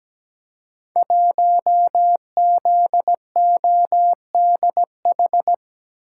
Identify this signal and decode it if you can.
Morse code, heard as a keyed tone, about 17 wpm, 700 Hz. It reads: 1ZODH
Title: Code Groups - 5 Letter:Numbers - 17wpm